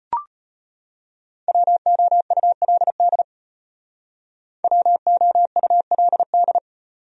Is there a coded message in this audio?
Yes: Morse code from 1.5 to 6.6 s, heard as a keyed tone, about 32 wpm, 700 Hz. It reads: WOULD WOULD